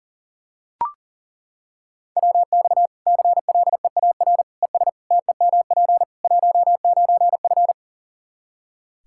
Morse code, heard as a keyed tone, about 40 wpm, 700 Hz. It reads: WX CLEAR ES TEMP 19F
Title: Practice QSO - Exchange 2 - Info - 40wpm